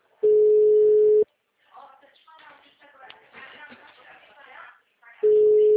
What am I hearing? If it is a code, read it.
Ringback tone (Europe, 425 Hz), with speech.
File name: out- -2005- - - .20932